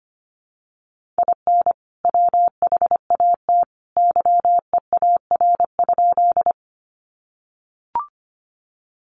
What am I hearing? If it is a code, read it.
Morse code, heard as a keyed tone, about 25 wpm, 700 Hz. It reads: IN WHAT YEAR?
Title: Sentences from Top 200 Words 25wpm